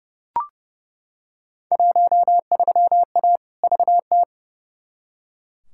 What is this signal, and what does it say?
Morse code, heard as a keyed tone, about 30 wpm, 700 Hz. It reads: 13A VT
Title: Field Day Exchanges Encore - 30wpm